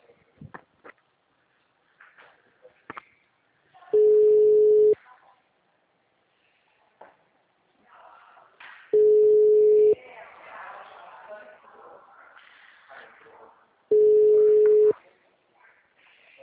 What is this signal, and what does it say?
Ringback tone (Europe, 425 Hz), with speech.
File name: out- -2010- - - .51